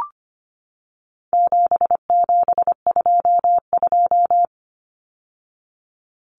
Morse code, heard as a keyed tone, about 25 wpm, 700 Hz. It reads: 7722
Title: Numbers - 4-Digit 25wpm